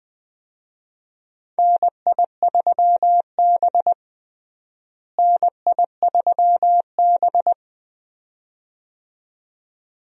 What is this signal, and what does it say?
Morse code, heard as a keyed tone, about 20 wpm, 700 Hz. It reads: NI3B NI3B
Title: USA Call Signs - 4 Characters - Repeated 2x - 20wpm